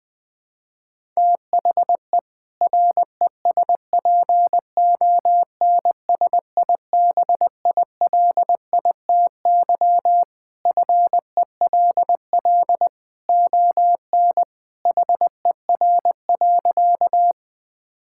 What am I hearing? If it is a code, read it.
Morse code, heard as a keyed tone, about 20 wpm, 700 Hz. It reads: THE RESPONSIBILITY FELL ON HER.
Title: Sentences with 2-Word Phrases - Spoken Beginning - 20wpm